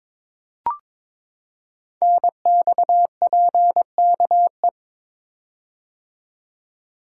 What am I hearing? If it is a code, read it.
Morse code, heard as a keyed tone, about 22 wpm, 700 Hz. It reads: NXPKE
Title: Code Groups - 5 Letters - 22wpm